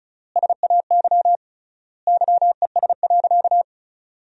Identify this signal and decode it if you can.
Morse code, heard as a keyed tone, about 34 wpm, 700 Hz. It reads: SAY YES.
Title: 3x Word-Spacing - Sentences from Top 500 Words - 35wpm